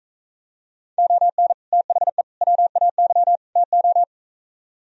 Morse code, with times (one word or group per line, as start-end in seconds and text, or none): 1.0-1.5 ON
1.7-2.2 THE
2.4-3.4 WAY
3.6-4.0 TO